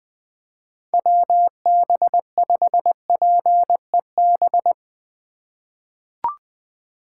Morse code, heard as a keyed tone, about 20 wpm, 700 Hz. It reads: WB5PEB
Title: USA Call Signs - 6 Characters - 2x Character Spacing - 20wpm